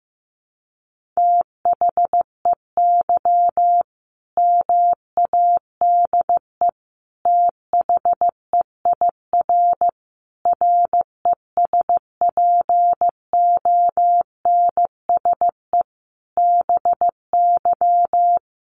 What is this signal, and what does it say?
Morse code, heard as a keyed tone, about 15 wpm, 700 Hz. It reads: THEY MADE THEIR RESPONSE BY
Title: Sentences from Top 1400 Words 15wpm